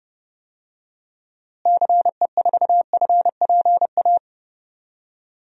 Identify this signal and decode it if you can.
Morse code, heard as a keyed tone, about 29 wpm, 700 Hz. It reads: CE4FPA
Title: DXCC-100 Call Signs - 6 Characters - Repeated 2x - 30wpm